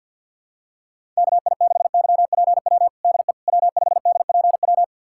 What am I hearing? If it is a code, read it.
Morse code, heard as a keyed tone, about 51 wpm, 700 Hz. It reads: KI6YPW DE W5DPW
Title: USA Callsign DE Callsign 50wpm